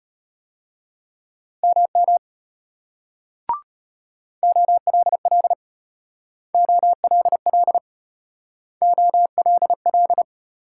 Morse code, heard as a keyed tone, about 35 wpm, 700 Hz. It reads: MM OLL OLL OLL